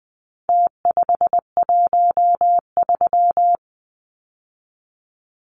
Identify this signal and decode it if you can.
Morse code, heard as a keyed tone, about 20 wpm, 700 Hz. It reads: T513
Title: Numbers - Contest Style - 4-Digit 20wpm